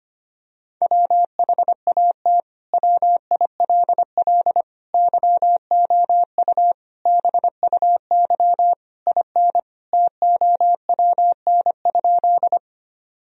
Morse code, heard as a keyed tone, about 25 wpm, 700 Hz. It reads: WHAT WILL YOU BUY IN TOWN?
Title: Sentences from Top 1000 Words - Review Part 4 - 25wpm